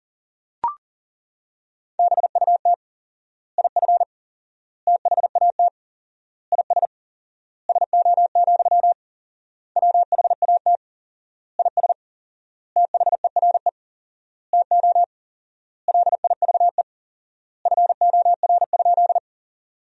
Morse code, heard as a keyed tone, about 40 wpm, 700 Hz. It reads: BUT IF THAT IS SO, WHAT IS THERE TO LIVE FOR?
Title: 4x Word-Spacing - Sentences from Top 200 Words - 40wpm